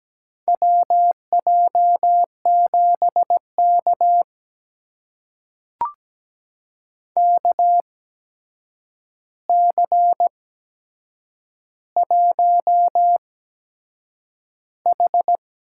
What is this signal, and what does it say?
Morse code, heard as a keyed tone, about 17 wpm, spaced as for 7 wpm, 700 Hz. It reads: WJ7K KC1H
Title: USA Call Signs - 4 Characters - 8x Character Spacing - 17wpm